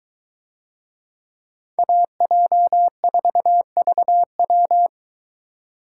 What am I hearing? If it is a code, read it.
Morse code, heard as a keyed tone, about 21 wpm, 700 Hz. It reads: AJ4VW